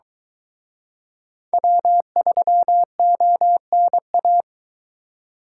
Morse code, heard as a keyed tone, about 20 wpm, 700 Hz. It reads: W3ONA
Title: Speed Racing - USA 1x3 Call Signs 17wpm